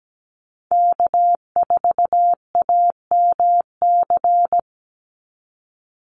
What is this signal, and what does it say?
Morse code, heard as a keyed tone, about 17 wpm, 700 Hz. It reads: K4AMC
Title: Call Signs Repeated 2x - 17wpm